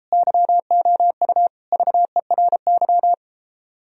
Morse code, heard as a keyed tone, about 37 wpm, 700 Hz. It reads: YOU VERY